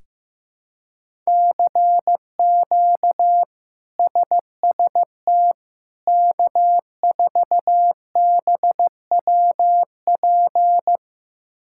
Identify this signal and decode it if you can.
Morse code, heard as a keyed tone, about 15 wpm, 700 Hz. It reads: CQ SST K4BWP
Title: K1USN SST QSO Practice - 15wpm